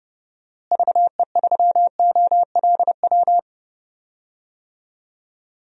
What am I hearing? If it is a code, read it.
Morse code, heard as a keyed tone, about 30 wpm, 700 Hz. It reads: VE3OLW
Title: DXCC-100 Call Signs - 6 Characters - Repeated 2x - 30wpm